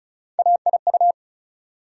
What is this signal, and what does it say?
Morse code, heard as a keyed tone, about 36 wpm, 700 Hz. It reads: AIU
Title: USA Call Sign Postfixes - 3 Letters 35wpm